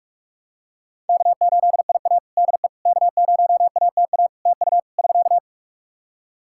Morse code, heard as a keyed tone, about 47 wpm, 700 Hz. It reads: K8IA DE K0ATA TU <SK>
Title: Practice QSO - Exchange 4 - Ending - 45wpm